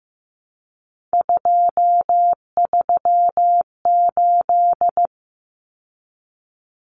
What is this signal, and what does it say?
Morse code, heard as a keyed tone, about 15 wpm, 700 Hz. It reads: 238
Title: Numbers - 3-Digit 15wpm